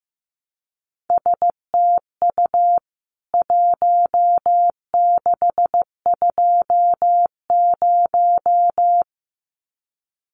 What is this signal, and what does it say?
Morse code, heard as a keyed tone, about 15 wpm, 700 Hz. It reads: STU 1620
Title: CWT Exchange 15wpm